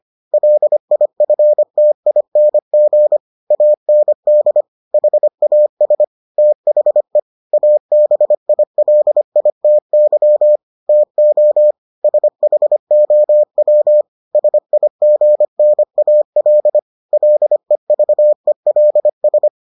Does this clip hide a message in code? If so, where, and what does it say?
Yes: Morse code from 0.3 to 19.5 s, heard as a keyed tone, about 25 wpm, 600 Hz. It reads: LIFTING AND HAS THE ABILITY TO SHOW SIGNAL LEVELS